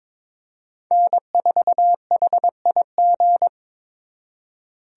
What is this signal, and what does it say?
Morse code, heard as a keyed tone, about 26 wpm, 700 Hz. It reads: N4HIG
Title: Speed Racing - USA Call Signs 22wpm